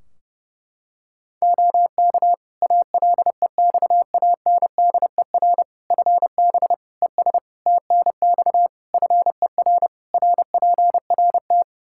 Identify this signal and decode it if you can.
Morse code, heard as a keyed tone, about 32 wpm, 700 Hz. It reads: OK ALEXANDER FB ES TNX FER RPRT
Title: K1USN SST QSO Practice 30wpm